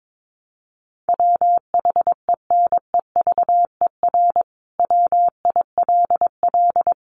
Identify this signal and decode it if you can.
Morse code, heard as a keyed tone, about 22 wpm, 700 Hz. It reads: WHENEVER WILL
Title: Sentences from Top 1200 Words 22wpm